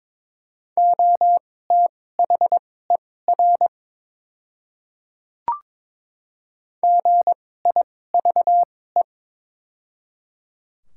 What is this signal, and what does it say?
Morse code, heard as a keyed tone, about 22 wpm, spaced as for 16 wpm, 700 Hz. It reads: OTHER GIVE
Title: 2x Character Spacing - Top 200 Words - 22wpm